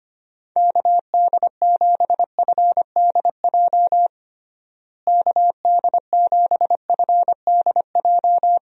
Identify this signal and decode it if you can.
Morse code, heard as a keyed tone, about 25 wpm, 700 Hz. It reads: KD7FDJ KD7FDJ
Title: USA Call Signs - 6 Characters - Repeated 3x - 25wpm